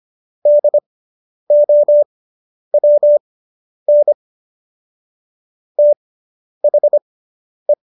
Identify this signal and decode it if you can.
Morse code, heard as a keyed tone, about 25 wpm, spaced as for 10 wpm, 590 Hz. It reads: DOWN THE